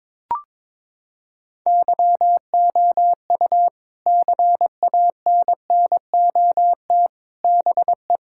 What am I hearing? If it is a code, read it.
Morse code, heard as a keyed tone, about 22 wpm, 700 Hz. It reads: YOU CANNOT BE
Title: Sentences from Top 1000 Words - Review Part 6 - 22wpm